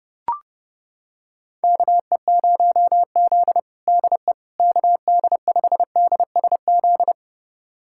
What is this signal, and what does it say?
Morse code, heard as a keyed tone, about 32 wpm, 700 Hz. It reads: KE0Z DE KD5DSZ
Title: USA Callsign DE Callsign 30wpm